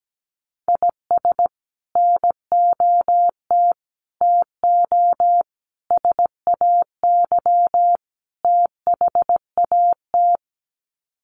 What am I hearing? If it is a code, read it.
Morse code, heard as a keyed tone, about 17 wpm, 700 Hz. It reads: IS NOT TO SAY THAT
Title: Sets of 5 Words 17wpm